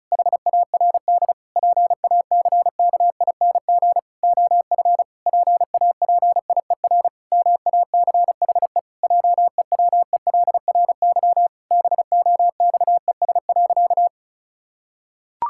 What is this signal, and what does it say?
Morse code, heard as a keyed tone, about 34 wpm, 700 Hz. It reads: HARD PACKING OF PAPIER MACHE JEWELRY BOXES.
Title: Pangram Challenge v2 - 35wpm